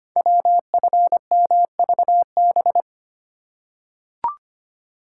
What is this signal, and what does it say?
Morse code, heard as a keyed tone, about 25 wpm, 700 Hz. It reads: WFMVB
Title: Code Groups - 5 Letters - 25wpm